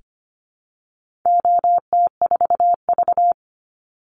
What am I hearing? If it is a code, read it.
Morse code, heard as a keyed tone, about 25 wpm, 700 Hz. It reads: OT4V
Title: Call Signs Repeated 3x - 25wpm